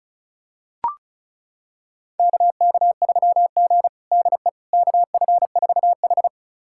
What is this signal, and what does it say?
Morse code, heard as a keyed tone, about 34 wpm, 700 Hz. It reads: KK3G DE KF4H
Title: USA Callsign DE Callsign - 4 Characters - 35wpm